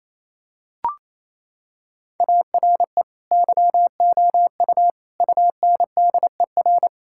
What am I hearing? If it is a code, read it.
Morse code, heard as a keyed tone, about 28 wpm, 700 Hz. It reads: ARE YOU UNDER
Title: Sentences from Top 1600 Words 28wpm